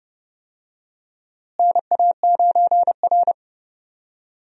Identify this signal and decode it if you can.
Morse code, heard as a keyed tone, about 30 wpm, 700 Hz. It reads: NA9R
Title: USA Call Signs - 4 Characters - Repeated 2x - 30wpm